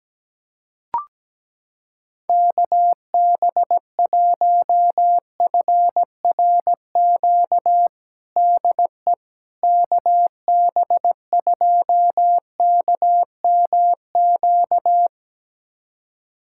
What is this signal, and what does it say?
Morse code, heard as a keyed tone, about 17 wpm, 700 Hz. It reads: KB1FRQ DE KB2KMQ
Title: USA Callsign DE Callsign - 6 Characters - 17wpm